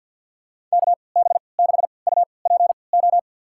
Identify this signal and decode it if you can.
Morse code, heard as a keyed tone, about 52 wpm, spaced as for 29 wpm, 700 Hz. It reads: KB6UPO
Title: USA Call Signs - 6 Characters - 3x Character Spacing - 50wpm